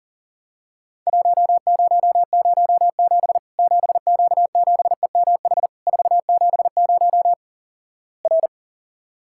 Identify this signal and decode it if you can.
Morse code, heard as a keyed tone, about 43 wpm, 700 Hz. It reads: 1007 7Q7EMH 470 R
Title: Worked All Europe DX Contest - QTC Exchange - 40wpm